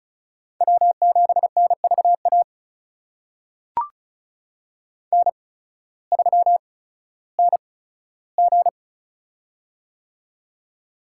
Morse code, heard as a keyed tone, about 36 wpm, spaced as for 16 wpm, 700 Hz. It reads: W7NVA N3NG
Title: USA Call Signs - 8x Character Spacing - 35wpm